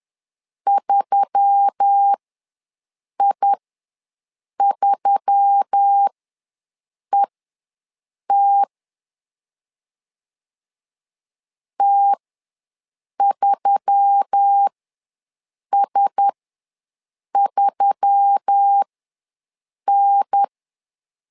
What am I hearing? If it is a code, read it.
Morse code, heard as a keyed tone, about 10 wpm, spaced as for 6 wpm, 800 Hz. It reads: 3I3ET T3S3N
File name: LEZIONI / 4 Q1S3 ws 10wpm 2437